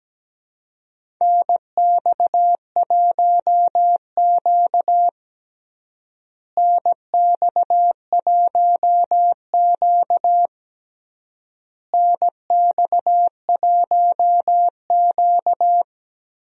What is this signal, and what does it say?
Morse code, heard as a keyed tone, about 17 wpm, 700 Hz. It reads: NX1Q NX1Q NX1Q
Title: USA Call Signs - 4 Characters - Repeated 3x - 17wpm